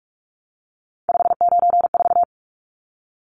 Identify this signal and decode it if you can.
Morse code, heard as a keyed tone, about 44 wpm, 700 Hz. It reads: 594
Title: Numbers - 3-Digit 45wpm